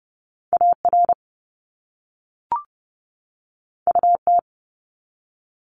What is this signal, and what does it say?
Morse code, heard as a keyed tone, about 32 wpm, 700 Hz. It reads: AR UT
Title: ARRL Sections 30wpm